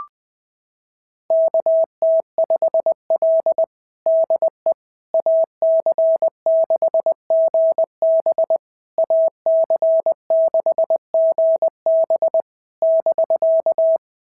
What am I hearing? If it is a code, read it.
Morse code, heard as a keyed tone, about 19 wpm, 650 Hz. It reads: KT5L DE AC6GB AC6GB <BK>
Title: SOTA QSO Practice 20wpm